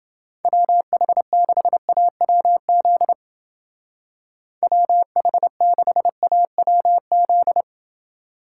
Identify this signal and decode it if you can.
Morse code, heard as a keyed tone, about 29 wpm, 700 Hz. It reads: WH6AWZ WH6AWZ